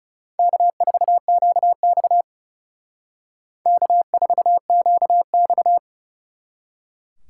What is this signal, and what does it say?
Morse code, heard as a keyed tone, about 36 wpm, 700 Hz. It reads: K4QX K4QX
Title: Speed Racing - USA 1x2 Call Signs 30wpm